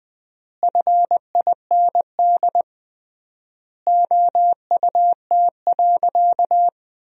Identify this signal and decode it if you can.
Morse code, heard as a keyed tone, about 20 wpm, 700 Hz. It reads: FIND OUT.
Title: 3x Word-Spacing - Sentences from Top 200 Words - 20wpm